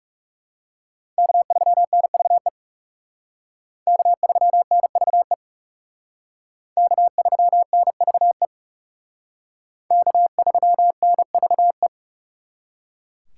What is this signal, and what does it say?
Morse code, heard as a keyed tone, about 37 wpm, 700 Hz. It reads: K3NVE K3NVE K3NVE K3NVE
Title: Speed Racing - USA 1x3 Call Signs 30wpm